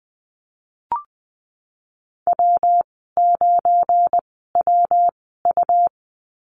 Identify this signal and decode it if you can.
Morse code, heard as a keyed tone, about 20 wpm, spaced as for 15 wpm, 700 Hz. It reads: W9WU
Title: 231 Lesson 37g - Letter x - Call Signs 20wpm